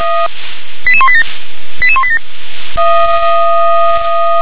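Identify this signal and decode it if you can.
Five-tone selective call (ZVEI), heard as quick runs of five tones, tones of about 70 ms. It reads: 88187 88187
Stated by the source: Five-tone ZVEI call 88186, received by SDR.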